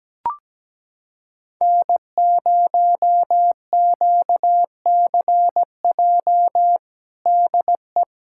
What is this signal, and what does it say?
Morse code, heard as a keyed tone, about 17 wpm, 700 Hz. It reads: N0QCJ DE
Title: K1USN SST QSO Practice 17wpm